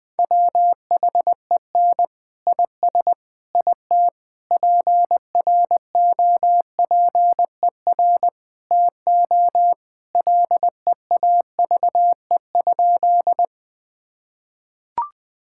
Morse code, heard as a keyed tone, about 21 wpm, 700 Hz. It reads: WHEN IS IT PROPER TO LEAVE?
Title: Sentences from Top 1000 Words - Review Part 5 - 20wpm